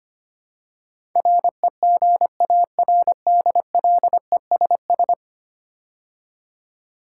Morse code, heard as a keyed tone, about 25 wpm, 700 Hz. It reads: REGARDLESS
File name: 10 Letter Words - First 3 Letters Given - 25wpm